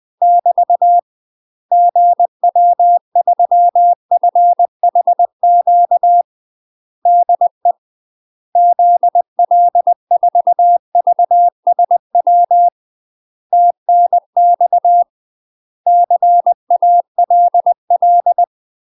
Morse code, heard as a keyed tone, about 20 wpm, 700 Hz. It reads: <BT> GW3FHQ DE ZL4VSW TNX CALL
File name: 20WPM